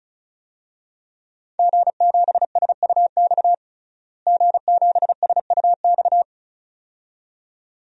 Morse code, heard as a keyed tone, about 36 wpm, 700 Hz. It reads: G7SUX G7SUX
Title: DXCC-100 Call Signs - 5 Characters - Repeated 2x - 35wpm